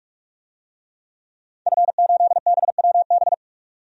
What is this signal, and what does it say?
Morse code, heard as a keyed tone, about 46 wpm, 700 Hz. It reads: F8BWB
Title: DXCC-100 Call Signs - 5 Characters - 45wpm